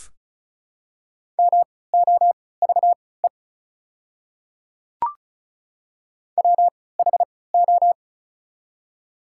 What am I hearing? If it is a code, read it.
Morse code, heard as a keyed tone, about 34 wpm, spaced as for 20 wpm, 690 Hz. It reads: MOVE WHO
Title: 3x Character Spacing - Top 200 Words - 35wpm